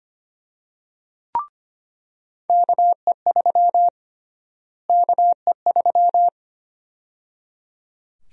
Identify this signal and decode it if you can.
Morse code, heard as a keyed tone, about 25 wpm, 700 Hz. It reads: KE3 KE3
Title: USA Call Sign Prefixes - Repeated 2x - 25wpm